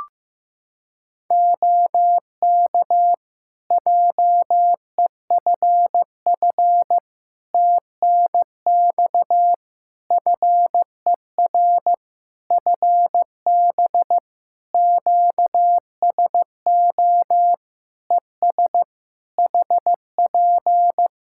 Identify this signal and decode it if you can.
Morse code, heard as a keyed tone, about 15 wpm, 700 Hz. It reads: OK JEFF TNX FER FB QSO ES HP
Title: Practice QSO - Exchange 4 - Ending - 15wpm